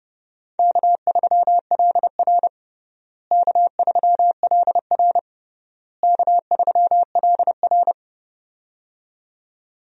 Morse code, heard as a keyed tone, about 30 wpm, 700 Hz. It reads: K3LR K3LR K3LR
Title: USA Call Signs - 4 Characters - Repeated 3x - 30wpm